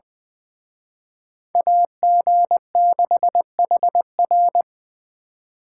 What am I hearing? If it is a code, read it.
Morse code, heard as a keyed tone, about 24 wpm, 700 Hz. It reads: AG6HR
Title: Speed Racing - USA 2x2 Call Signs 20wpm